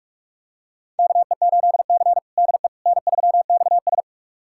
Morse code, heard as a keyed tone, about 46 wpm, 700 Hz. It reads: KE8C DE N3XS
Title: USA Callsign DE Callsign - 4 Characters - 45wpm